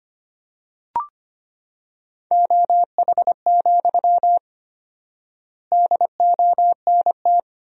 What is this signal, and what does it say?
Morse code, heard as a keyed tone, about 25 wpm, 700 Hz. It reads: OH, DONT
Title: 4x Word-Spacing - Sentences from Top 500 Words - 25wpm